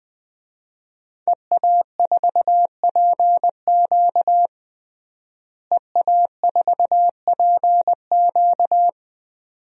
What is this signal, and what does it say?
Morse code, heard as a keyed tone, about 21 wpm, 700 Hz. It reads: EA4PQ EA4PQ
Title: DXCC-100 Call Signs - 5 Characters - Repeated 2x - 20wpm